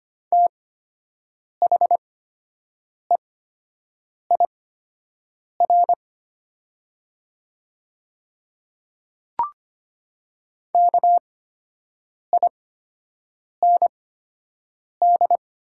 Morse code, heard as a keyed tone, about 25 wpm, spaced as for 7 wpm, 700 Hz. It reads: THEIR KIND
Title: 8x Character Spacing - Top 200 Words - 25wpm